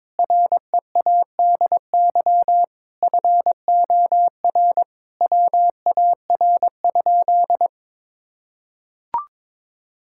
Morse code, heard as a keyed tone, about 22 wpm, 700 Hz. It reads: READY FOR WAR?
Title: Sentences from Top 1000 Words - Review Part 5 - 22wpm